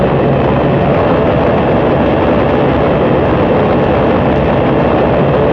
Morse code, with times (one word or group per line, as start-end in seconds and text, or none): none